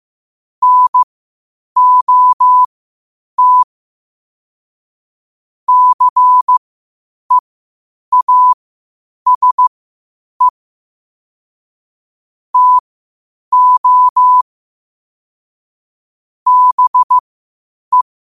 Morse code, heard as a keyed tone, about 15 wpm, spaced as for 8 wpm, 1000 Hz. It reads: NOT CEASE TO BE
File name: qotdfiles / 8wpm